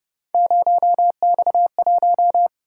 Morse code, heard as a keyed tone, about 30 wpm, 700 Hz. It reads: 0X1